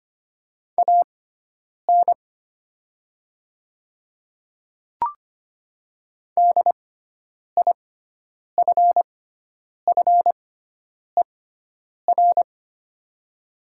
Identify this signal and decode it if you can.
Morse code, heard as a keyed tone, about 25 wpm, spaced as for 9 wpm, 700 Hz. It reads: AN DIFFER
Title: 6x Character Spacing - Top 200 Words - 25wpm